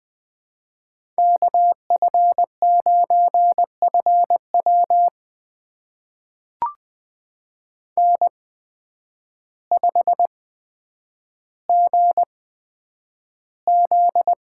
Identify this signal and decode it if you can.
Morse code, heard as a keyed tone, about 21 wpm, spaced as for 9 wpm, 700 Hz. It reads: KF9FW N5GZ
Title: USA Call Signs - 5 Characters - 8x Character Spacing - 20wpm